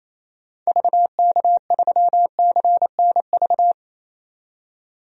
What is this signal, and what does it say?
Morse code, heard as a keyed tone, about 28 wpm, 700 Hz. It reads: VK3CNV
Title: DXCC-100 Call Signs - 6 Characters - 28wpm